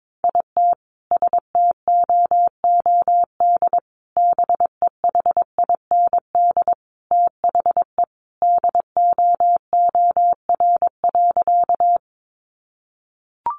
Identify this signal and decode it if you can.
Morse code, heard as a keyed tone, about 22 wpm, 700 Hz. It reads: IT STOOD BEHIND THE DOOR.
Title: Sentences from Top 500 Words 22wpm